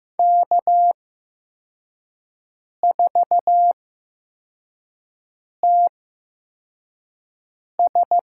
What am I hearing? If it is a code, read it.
Morse code, heard as a keyed tone, about 15 wpm, spaced as for 4 wpm, 700 Hz. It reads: K4TS
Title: USA Call Signs - 4 Characters - 8x Character Spacing - 15wpm